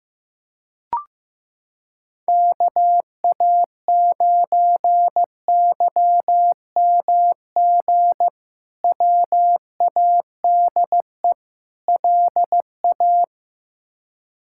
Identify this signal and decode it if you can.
Morse code, heard as a keyed tone, about 16 wpm, 700 Hz. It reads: KA9YMG WADE LA
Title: K1USN SST QSO Practice - 15wpm